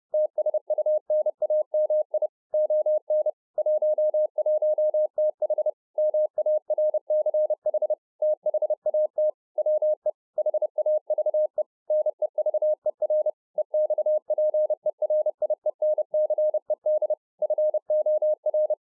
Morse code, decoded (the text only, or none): TSUNAMI ON 11TH MARCH THAT WE HAVE NEVER EXPERIENCED FOR